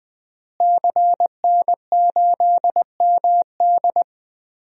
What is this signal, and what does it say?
Morse code, heard as a keyed tone, about 21 wpm, 700 Hz. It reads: CN8MD
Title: DXCC-100 Call Signs - 5 Characters - Repeated 2x - 20wpm